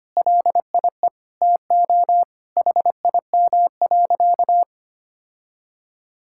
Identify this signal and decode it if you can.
Morse code, heard as a keyed tone, about 25 wpm, 700 Hz. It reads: LIE TO HIM.
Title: Sentences from Top 1600 Words 25wpm